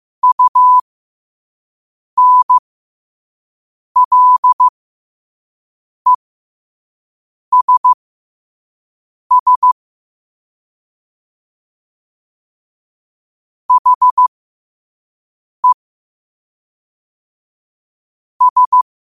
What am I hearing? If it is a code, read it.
Morse code, heard as a keyed tone, about 15 wpm, spaced as for 5 wpm, 1000 Hz. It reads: UNLESS HE S
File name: qotdfiles / 5wpm